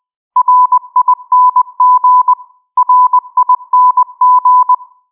Morse code, heard as a keyed tone, about 20 wpm, 1000 Hz. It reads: RING RING